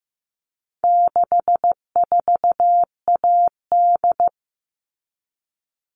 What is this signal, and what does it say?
Morse code, heard as a keyed tone, about 15 wpm, 700 Hz. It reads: 64AD
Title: All Cut Numbers - 4-Digit 15wpm